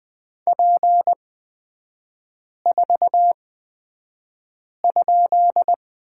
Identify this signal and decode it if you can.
Morse code, heard as a keyed tone, about 20 wpm, spaced as for 5 wpm, 690 Hz. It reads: P4?
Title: 236 Lesson 38e - Letter q - Three Characters 20wpm